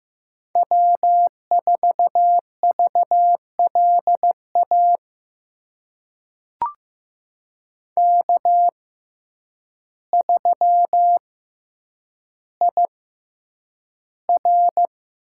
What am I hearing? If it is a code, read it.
Morse code, heard as a keyed tone, about 15 wpm, spaced as for 8 wpm, 700 Hz. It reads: W4VLA K3IR
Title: USA Call Signs - 5 Characters - 6x Character Spacing - 15wpm